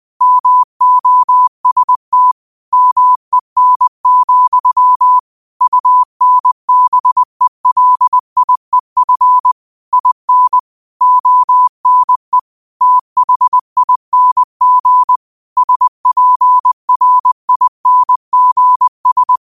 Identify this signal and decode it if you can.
Morse code, heard as a keyed tone, about 20 wpm, 1000 Hz. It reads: MOST MEN, UNBELIEF IN ONE THING SPRINGS
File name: qotdfiles / 20wpm